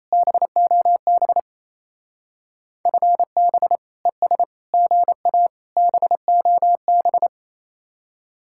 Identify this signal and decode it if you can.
Morse code, heard as a keyed tone, about 34 wpm, 700 Hz. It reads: BOB FB ES GA BOB
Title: Speed-Racing Common QSO Phrases - 28wpm